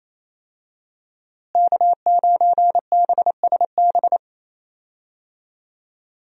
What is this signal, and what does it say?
Morse code, heard as a keyed tone, about 27 wpm, 700 Hz. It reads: K9BSB
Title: USA Call Signs - 5 Characters - Repeated 3x - 28wpm